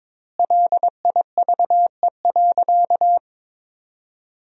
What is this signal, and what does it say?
Morse code, heard as a keyed tone, about 22 wpm, 700 Hz. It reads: LIVE.
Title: 3x Word-Spacing - Sentences from Top 300 Words - 22wpm